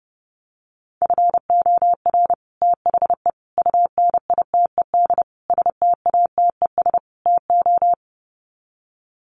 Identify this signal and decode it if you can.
Morse code, heard as a keyed tone, about 30 wpm, 700 Hz. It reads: FOR THE UNITED STATES TO
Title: Sets of 5 Words Encore 30wpm